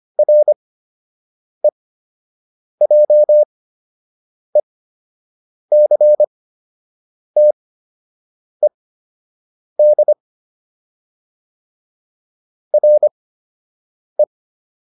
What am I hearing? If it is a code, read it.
Morse code, heard as a keyed tone, about 25 wpm, spaced as for 7 wpm, 600 Hz. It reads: REJECTED RE
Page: fundamental-practice-files